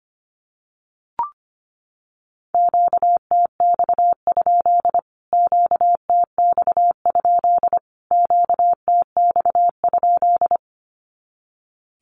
Morse code, heard as a keyed tone, about 25 wpm, 700 Hz. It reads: QTX? QTX? QTX?
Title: Q-Codes Repeated 3x 25wpm